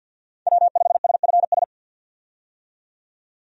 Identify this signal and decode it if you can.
Morse code, heard as a keyed tone, about 48 wpm, 700 Hz. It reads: W5SFS
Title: USA Call Signs - 5 Characters - 50wpm